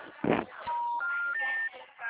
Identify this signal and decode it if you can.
Special information tone, with speech.